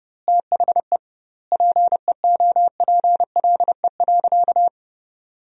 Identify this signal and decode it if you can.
Morse code, heard as a keyed tone, about 32 wpm, 700 Hz. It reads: THE PEOPLE.
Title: 2x Word-Spacing - Sentences from Top 400 Words - 30wpm